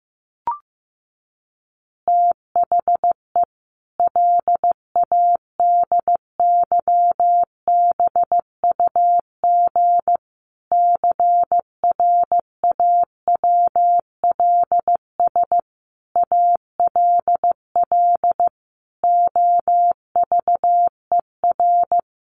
Morse code, heard as a keyed tone, about 15 wpm, 700 Hz. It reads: THE LADYBUG CRAWLS ALL OVER
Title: Additional Easy Phrases 15wpm